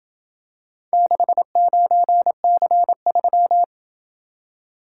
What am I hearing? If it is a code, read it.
Morse code, heard as a keyed tone, about 28 wpm, 700 Hz. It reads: 69C3